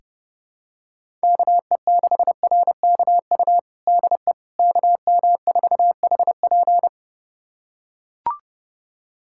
Morse code, heard as a keyed tone, about 30 wpm, 700 Hz. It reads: KE6RKU DE KM4HP
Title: USA Callsign DE Callsign 30wpm